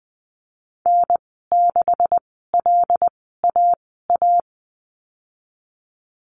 Morse code, heard as a keyed tone, about 20 wpm, spaced as for 15 wpm, 700 Hz. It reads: N6LAA
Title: 173 Lesson 28g - Number 9 - Call Signs 20wpm